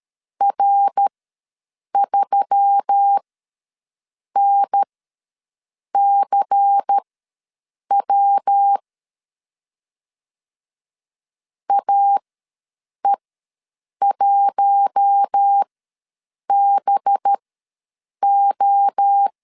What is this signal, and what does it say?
Morse code, heard as a keyed tone, about 13 wpm, spaced as for 7 wpm, 800 Hz. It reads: R3NCW AE1BO
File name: LEZIONI / 6 PKX0 / 6 PK ws 12wpm 3116